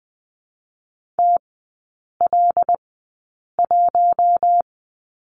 Morse code, heard as a keyed tone, about 19 wpm, spaced as for 8 wpm, 700 Hz. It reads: TL1
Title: 166 Lesson 27e - Number 7 - Three Characters 20wpm